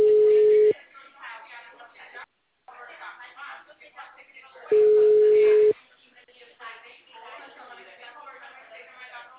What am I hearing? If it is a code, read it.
Ringback tone (Europe, 425 Hz), with speech.